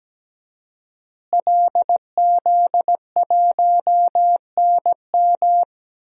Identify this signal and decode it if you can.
Morse code, heard as a keyed tone, about 17 wpm, 700 Hz. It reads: LZ1NM
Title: DXCC-100 Call Signs - 5 Characters - Repeated 3x - 17wpm